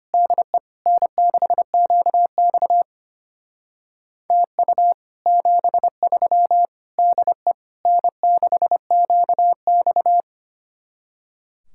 Morse code, heard as a keyed tone, about 33 wpm, 700 Hz. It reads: DE N6QX TU 73 DE N6QX